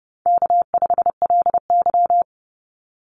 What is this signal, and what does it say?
Morse code, heard as a keyed tone, about 28 wpm, 700 Hz. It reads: K5LY
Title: Easy Call Signs Repeated 2x - 30wpm